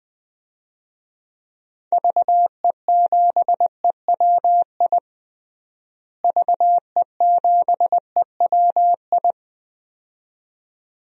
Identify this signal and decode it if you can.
Morse code, heard as a keyed tone, about 20 wpm, 700 Hz. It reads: VE7EWI VE7EWI
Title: DXCC-100 Call Signs - 6 Characters - Repeated 2x - 20wpm